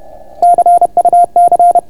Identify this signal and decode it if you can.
Morse code, heard as a keyed tone, about 31 wpm, 690 Hz. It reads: CUC